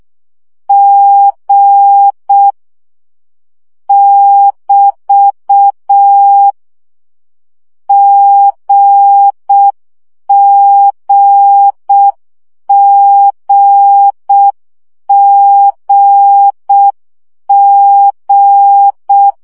Morse code, heard as a keyed tone, about 6 wpm, 800 Hz. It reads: G <BT> GGGGG